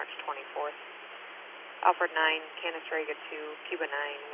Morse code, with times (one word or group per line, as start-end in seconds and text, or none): none